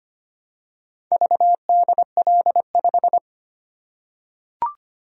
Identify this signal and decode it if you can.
Morse code, heard as a keyed tone, about 25 wpm, 700 Hz. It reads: VDL5
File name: practice / 25wpm / Four Letter-Number Combinations - 25wpm